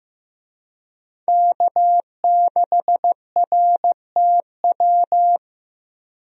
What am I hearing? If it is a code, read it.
Morse code, heard as a keyed tone, about 15 wpm, 700 Hz. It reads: K6RTW
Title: USA Call Signs - 5 Characters - Repeated 2x - 15wpm